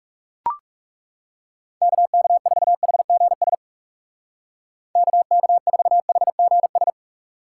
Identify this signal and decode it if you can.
Morse code, heard as a keyed tone, about 35 wpm, 700 Hz. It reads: KK4HGS KK4HGS
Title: Speed Racing - USA 2x3 Call Signs 30wpm